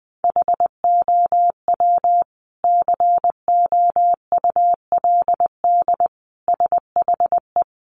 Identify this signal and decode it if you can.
Morse code, heard as a keyed tone, about 20 wpm, 700 Hz. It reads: HOW COULD SHE
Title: Sentences from Top 1300 Words 20wpm